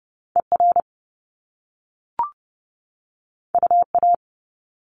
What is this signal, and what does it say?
Morse code, heard as a keyed tone, about 30 wpm, 700 Hz. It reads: ER UA
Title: Top Two Letter Combinations in Words 30wpm